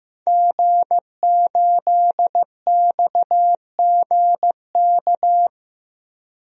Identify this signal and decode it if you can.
Morse code, heard as a keyed tone, about 15 wpm, 700 Hz. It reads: G8XGK
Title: DXCC-100 Call Signs - 5 Characters - 15wpm